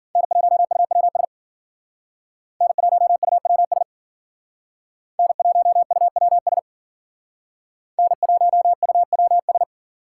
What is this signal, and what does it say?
Morse code, heard as a keyed tone, about 52 wpm, 700 Hz. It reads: N1UWS N1UWS N1UWS N1UWS